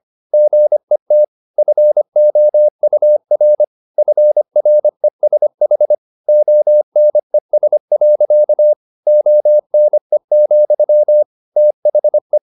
Morse code, heard as a keyed tone, about 25 wpm, 600 Hz. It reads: GET FOUR FRESH ONES. ONE, THE
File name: the old man and the sea part 1 25wpm